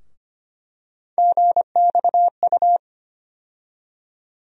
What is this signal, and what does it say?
Morse code, heard as a keyed tone, about 25 wpm, 700 Hz. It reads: GXU
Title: USA Call Sign Postfixes - 3 Letters 25wpm